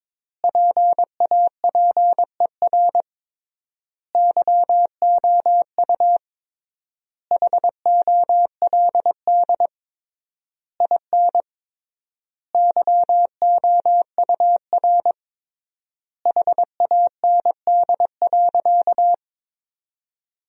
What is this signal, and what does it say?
Morse code, heard as a keyed tone, about 22 wpm, 700 Hz. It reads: PAPER YOU HOLD IN YOUR HAND.
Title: 3x Word-Spacing - Sentences from Top 400 Words - 22wpm